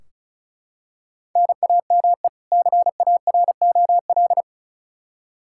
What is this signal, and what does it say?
Morse code, heard as a keyed tone, about 34 wpm, 700 Hz. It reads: NAME CAROL
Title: Practice QSO - Exchange 1 - Beginning - 35wpm